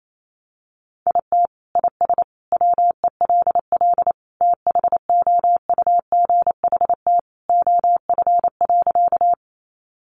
Morse code, heard as a keyed tone, about 28 wpm, 700 Hz. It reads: IT IS WELL THOUGHT OF.
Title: Sentences from Top 300 Words 28wpm